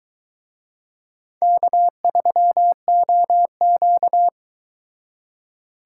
Morse code, heard as a keyed tone, about 22 wpm, 700 Hz. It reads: K3OQ